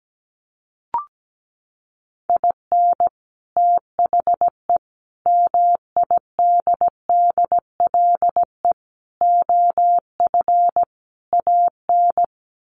Morse code, heard as a keyed tone, about 17 wpm, 700 Hz. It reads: IN THE MIDDLE OF AN
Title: Sets of 5 Words Encore2 17wpm